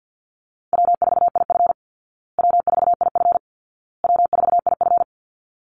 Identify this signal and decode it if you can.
Morse code, heard as a keyed tone, about 50 wpm, 700 Hz. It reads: W4IF W4IF W4IF
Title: Easy Call Signs Repeated 3x - 50wpm